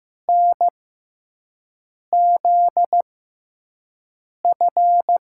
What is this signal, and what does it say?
Morse code, heard as a keyed tone, about 15 wpm, spaced as for 5 wpm, 700 Hz. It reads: NZF
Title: USA Call Signs - 6 Characters - 6x Character Spacing - 15wpm